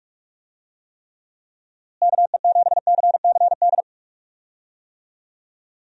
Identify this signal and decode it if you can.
Morse code, heard as a keyed tone, about 46 wpm, 700 Hz. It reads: KE7CCD
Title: USA Call Signs - 6 Characters - Repeated 2x - 45wpm